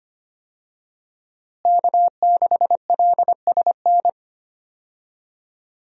Morse code, heard as a keyed tone, about 25 wpm, 700 Hz. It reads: K6LSN
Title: USA Call Signs - 5 Characters - Repeated 2x - 25wpm